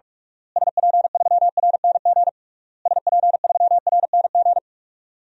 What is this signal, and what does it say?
Morse code, heard as a keyed tone, about 48 wpm, 700 Hz. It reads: SP3RNG SP3RNG
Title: DXCC-100 Call Signs - 6 Characters - Repeated 3x - 45wpm